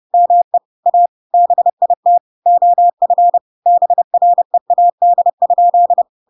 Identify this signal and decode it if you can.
Morse code, heard as a keyed tone, about 30 wpm, 700 Hz. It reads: ME A BIT OF BREAD?
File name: Pinocchio / Pinocchio ch6 30 30 ik1hge